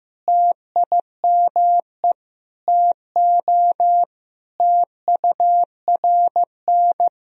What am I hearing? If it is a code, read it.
Morse code, heard as a keyed tone, about 15 wpm, 700 Hz. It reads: TIME TO TURN
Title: Sentences from Top 1000 Words - Review Part 1 - 15wpm